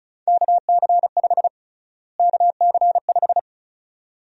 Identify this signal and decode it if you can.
Morse code, heard as a keyed tone, about 34 wpm, 700 Hz. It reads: KC5 KC5
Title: USA Call Sign Prefixes - Repeated 2x - 35wpm